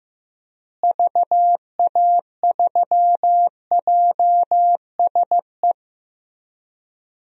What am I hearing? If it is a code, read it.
Morse code, heard as a keyed tone, about 15 wpm, 700 Hz. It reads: VA3JSE
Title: DXCC-100 Call Signs - 6 Characters - Repeated 3x - 15wpm